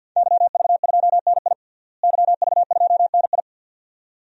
Morse code, heard as a keyed tone, about 47 wpm, 700 Hz. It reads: YV2NI YV2NI
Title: DXCC-100 Call Signs - 5 Characters - Repeated 2x - 50wpm